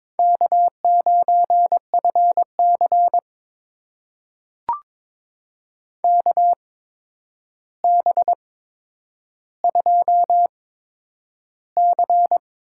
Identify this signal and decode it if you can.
Morse code, heard as a keyed tone, about 21 wpm, spaced as for 9 wpm, 700 Hz. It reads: K9FC KB2C
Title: USA Call Signs - 8x Character Spacing - 22wpm